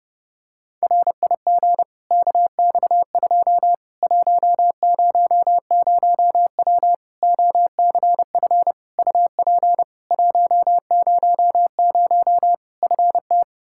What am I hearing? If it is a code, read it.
Morse code, heard as a keyed tone, about 30 wpm, 700 Hz. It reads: RIG KX2 100W OCF UP 100 FT
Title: Practice QSO - Exchange 2 - Info - 30wpm